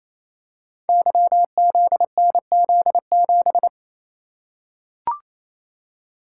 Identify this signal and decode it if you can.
Morse code, heard as a keyed tone, about 28 wpm, 700 Hz. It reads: YZNZ7
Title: Code Groups - 5 Letter:Numbers - 28wpm